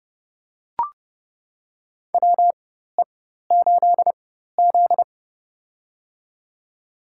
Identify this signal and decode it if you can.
Morse code, heard as a keyed tone, about 31 wpm, spaced as for 14 wpm, 700 Hz. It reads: WE8Z
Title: USA Call Signs - 4x Character Spacing - 30wpm